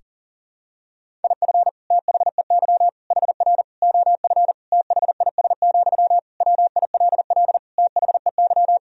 Morse code, heard as a keyed tone, about 40 wpm, 700 Hz. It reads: IF THEY HR OF THIS, WILL THEY
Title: Sentences from Top 400 Words and Abbreviations 40wpm